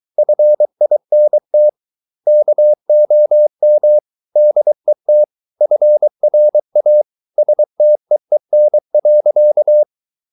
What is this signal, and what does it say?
Morse code, heard as a keyed tone, about 23 wpm, 600 Hz. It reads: FINT KOM DET FRA STEEN.
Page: Ferie 23 wpm